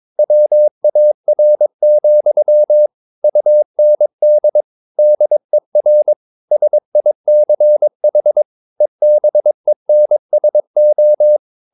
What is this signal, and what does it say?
Morse code, heard as a keyed tone, about 22 wpm, 600 Hz. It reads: WAR, UND DER SICH EBENSO